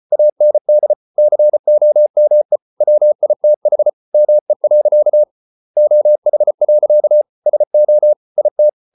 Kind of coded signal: Morse code